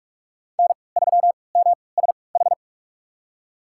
Morse code, heard as a keyed tone, about 46 wpm, spaced as for 26 wpm, 700 Hz. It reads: N3MSH
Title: USA Call Signs - 5 Characters - 3x Character Spacing - 45wpm